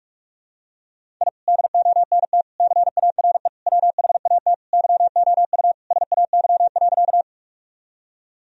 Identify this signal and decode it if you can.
Morse code, heard as a keyed tone, about 48 wpm, 700 Hz. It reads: I DONT CARE WHAT YOU SAY.
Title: Sentences from Top 1000 Words - Review Part 1 - 45wpm